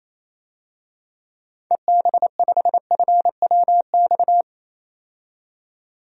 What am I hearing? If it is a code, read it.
Morse code, heard as a keyed tone, about 28 wpm, 700 Hz. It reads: EB5FWX
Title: DXCC-100 Call Signs - 6 Characters - 28wpm